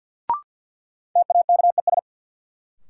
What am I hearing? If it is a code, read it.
Morse code, heard as a keyed tone, about 52 wpm, 700 Hz. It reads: TAKES
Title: Sentences from Top 1000 Words - Review Part 6 - New Word Drill - 50wpm